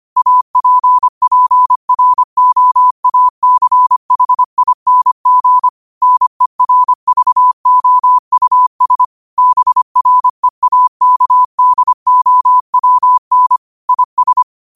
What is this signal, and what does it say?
Morse code, heard as a keyed tone, about 25 wpm, 1000 Hz. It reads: APPROACHING NERVOUS BREAKDOWN IS
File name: qotdfiles / 25wpm